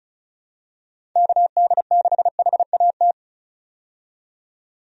Morse code, heard as a keyed tone, about 36 wpm, 700 Hz. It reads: KD6HAT
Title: USA Call Signs - 6 Characters - 35wpm